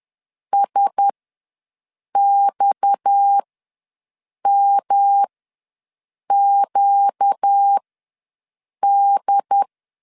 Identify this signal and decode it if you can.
Morse code, heard as a keyed tone, about 10 wpm, spaced as for 6 wpm, 800 Hz. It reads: SXMQD
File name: LEZIONI / 6 PKX0 / 6 PKX ws 10wpm 3706